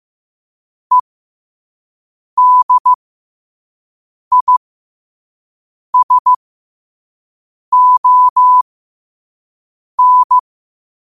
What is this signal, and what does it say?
Morse code, heard as a keyed tone, about 15 wpm, spaced as for 5 wpm, 1000 Hz. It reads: EDISON
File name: qotdfiles / 5wpm